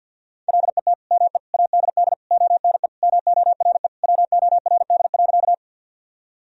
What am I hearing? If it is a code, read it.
Morse code, heard as a keyed tone, about 52 wpm, 700 Hz. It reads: LET ME ADD ONE MORE WORD.
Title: Sentences from Top 1000 Words - Review Part 1 - 50wpm